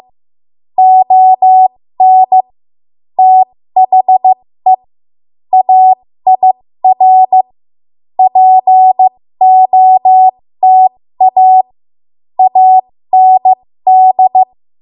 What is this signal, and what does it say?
Morse code, heard as a keyed tone, about 15 wpm, spaced as for 13 wpm, 750 Hz. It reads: ON THE AIR POTA AND
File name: Morse / 13WPM